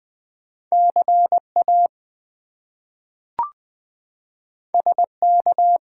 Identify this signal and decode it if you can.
Morse code, heard as a keyed tone, about 20 wpm, 700 Hz. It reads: CA SK
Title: US States and Canadian Provinces 20wpm